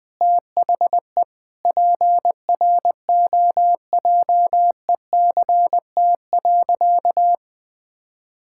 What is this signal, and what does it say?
Morse code, heard as a keyed tone, about 20 wpm, 700 Hz. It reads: THE PROJECT.
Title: Sentences with 6-Word Phrases - Spoken Beginning - 20wpm